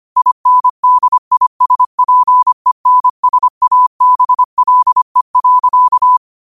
Morse code, heard as a keyed tone, about 25 wpm, 1000 Hz. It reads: INDISPENSABLE.
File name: qotdfiles / 25wpm